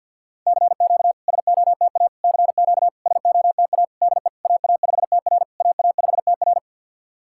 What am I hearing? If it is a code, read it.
Morse code, heard as a keyed tone, about 50 wpm, 700 Hz. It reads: CQ SOTA CQ SOTA DE AA5TR AA5TR
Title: SOTA QSO Practice 50wpm